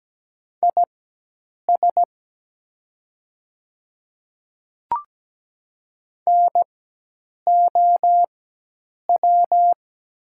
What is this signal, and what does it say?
Morse code, heard as a keyed tone, about 17 wpm, spaced as for 8 wpm, 700 Hz. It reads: IS NOW
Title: 4x Character Spacing - Top 100 Words - 17wpm